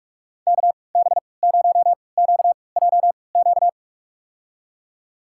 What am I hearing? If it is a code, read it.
Morse code, heard as a keyed tone, about 46 wpm, spaced as for 26 wpm, 700 Hz. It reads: KB0QJQ
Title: USA Call Signs - 6 Characters - 3x Character Spacing - 45wpm